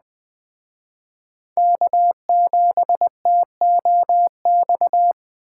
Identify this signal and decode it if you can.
Morse code, heard as a keyed tone, about 21 wpm, 700 Hz. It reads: K7TOX
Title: Speed Racing - USA 1x3 Call Signs 17wpm